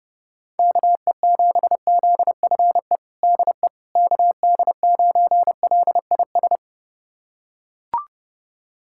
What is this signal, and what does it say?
Morse code, heard as a keyed tone, about 30 wpm, 700 Hz. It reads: KE7ZFE DE KD9LIS
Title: USA Callsign DE Callsign 30wpm